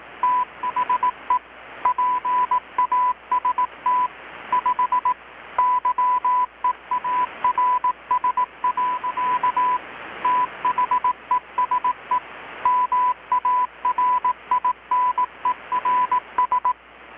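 Morse code, heard as a keyed tone, about 18 wpm, 1000 Hz. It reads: THE PAST 5 YEARS. THESE MARINERS
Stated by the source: Morse from KPH's Night of Nights XX maritime radiotelegraphy event, received in Chiapas, Mexico about 3,800 km away on a Kenwood R-600 with a grounded 10 m long-wire antenna.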